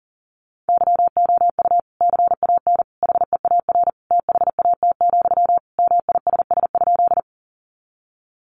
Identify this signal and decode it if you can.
Morse code, heard as a keyed tone, about 40 wpm, 700 Hz. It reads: YOU CAN HEAR THAT, MISS?
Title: Sentences from Top 500 Words 40wpm